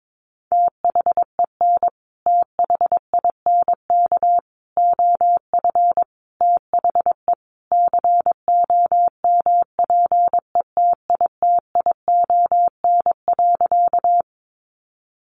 Morse code, heard as a keyed tone, about 22 wpm, 700 Hz. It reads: THEN THINK OF THE COMPETITION.
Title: Sentences from Top 1000 Words 22wpm